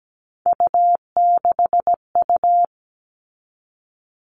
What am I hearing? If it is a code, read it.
Morse code, heard as a keyed tone, about 17 wpm, 700 Hz. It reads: U6U
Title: All Cut Numbers - 3-Digit 17wpm